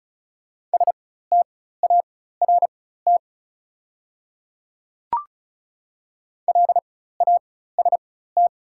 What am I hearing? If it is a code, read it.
Morse code, heard as a keyed tone, about 36 wpm, spaced as for 16 wpm, 690 Hz. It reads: START LAST
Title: 4x Character Spacing - Top 300 Words - 35wpm